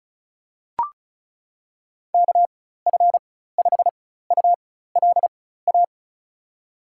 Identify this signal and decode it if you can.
Morse code, heard as a keyed tone, about 36 wpm, spaced as for 16 wpm, 700 Hz. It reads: KF5ULA
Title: USA Call Signs - 6 Characters - 4x Character Spacing - 35wpm